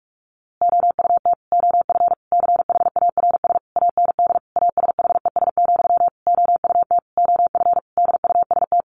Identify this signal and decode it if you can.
Morse code, heard as a keyed tone, about 44 wpm, 700 Hz. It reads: OUT OF CHARS AND ASHES, OUT OF DUST
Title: A Sound Of Thunder 45wpm